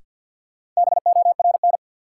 Morse code, heard as a keyed tone, about 50 wpm, 700 Hz. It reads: BORN
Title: Sentences from Top 1000 Words - New Word Drill - 50wpm